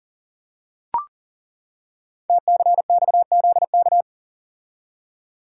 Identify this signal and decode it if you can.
Morse code, heard as a keyed tone, about 38 wpm, 690 Hz. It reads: TCXZK
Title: Code Groups - 5 Letters - 40wpm